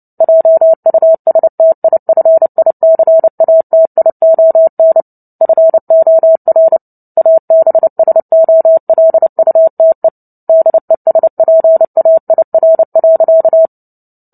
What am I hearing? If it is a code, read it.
Morse code, heard as a keyed tone, about 30 wpm, 650 Hz. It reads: JUSTIFICATION FOR ABSOLUTE DESPAIR.
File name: Book Two / 03-Chapter 3 - 29 WPM